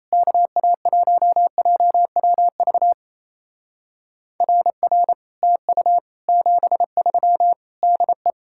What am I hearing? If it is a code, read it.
Morse code, heard as a keyed tone, about 35 wpm, 700 Hz. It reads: KA1JWV RR TU 73 DE